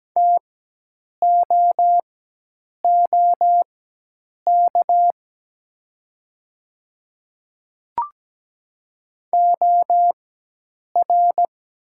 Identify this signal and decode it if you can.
Morse code, heard as a keyed tone, about 17 wpm, spaced as for 8 wpm, 700 Hz. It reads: TOOK OR
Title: 4x Character Spacing - Top 300 Words - 17wpm